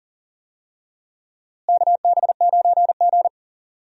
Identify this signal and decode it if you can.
Morse code, heard as a keyed tone, about 37 wpm, 700 Hz. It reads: KB9G
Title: USA Call Signs - 4 Characters - Repeated 2x - 40wpm